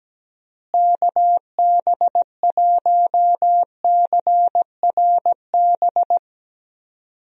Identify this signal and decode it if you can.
Morse code, heard as a keyed tone, about 17 wpm, 700 Hz. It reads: KB1CRB
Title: USA Call Signs - 6 Characters - Repeated 2x - 17wpm